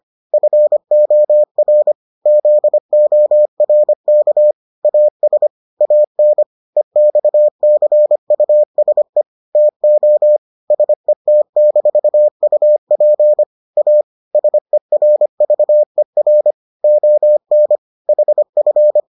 Morse code, heard as a keyed tone, about 25 wpm, 600 Hz. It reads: FOR ZORK AS AN EXCUSE TO SET-UP A SERVER ON HF